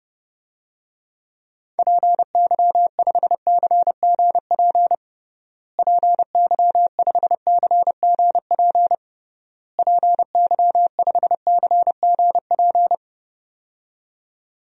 Morse code, heard as a keyed tone, about 30 wpm, 700 Hz. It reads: PY5CGP PY5CGP PY5CGP
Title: DXCC-100 Call Signs - 6 Characters - Repeated 3x - 30wpm